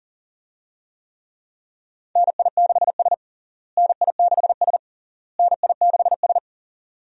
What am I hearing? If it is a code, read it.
Morse code, heard as a keyed tone, about 40 wpm, 700 Hz. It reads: NI6S NI6S NI6S
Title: USA Call Signs - 4 Characters - Repeated 3x - 40wpm